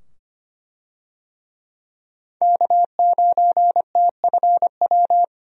Morse code, heard as a keyed tone, about 25 wpm, 700 Hz. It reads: K9TFW
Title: USA Call Signs - 5 Characters - 25wpm